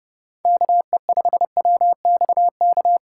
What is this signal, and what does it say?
Morse code, heard as a keyed tone, about 30 wpm, 700 Hz. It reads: KE5WXK
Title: USA Call Signs - 6 Characters - 3x Character Spacing - 30wpm